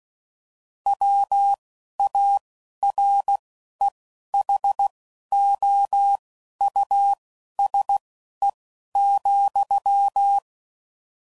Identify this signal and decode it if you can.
Morse code, heard as a keyed tone, about 16 wpm, spaced as for 11 wpm, 800 Hz. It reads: WAREHOUSE,
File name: Suffixes / General / Houses